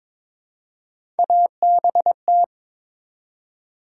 Morse code, heard as a keyed tone, about 22 wpm, 700 Hz. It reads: ABT
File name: Top 90 QSO Elements - 3x Character Spacing - 22wpm